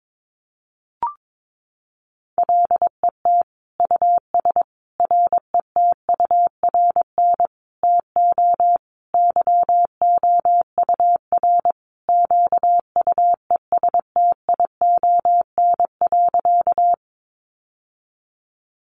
Morse code, heard as a keyed tone, about 22 wpm, 700 Hz. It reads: LET US RETURN TO YOUR QUESTION.
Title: Sentences from Top 800 Words 22wpm